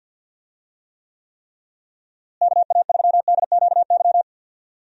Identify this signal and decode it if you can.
Morse code, heard as a keyed tone, about 51 wpm, 700 Hz. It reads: KA3DQY
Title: USA Call Signs - 6 Characters - Repeated 3x - 50wpm